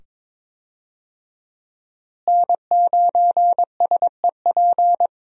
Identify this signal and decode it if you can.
Morse code, heard as a keyed tone, about 22 wpm, 700 Hz. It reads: N9SEP
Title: USA Call Signs - 5 Characters - Repeated 2x - 22wpm